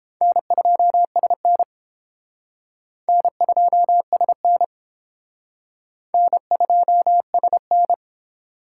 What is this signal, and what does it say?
Morse code, heard as a keyed tone, about 31 wpm, 700 Hz. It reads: N2SN N2SN N2SN